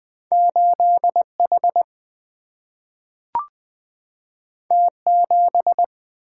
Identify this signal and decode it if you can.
Morse code, heard as a keyed tone, about 20 wpm, 700 Hz. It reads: 8H T7
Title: Single-Letter Single-Number 20wpm